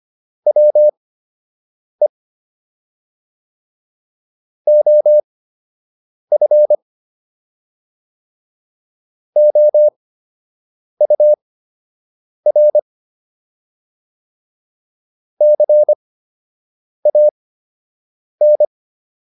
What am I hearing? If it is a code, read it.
Morse code, heard as a keyed tone, about 25 wpm, spaced as for 7 wpm, 600 Hz. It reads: WE OF OUR CAN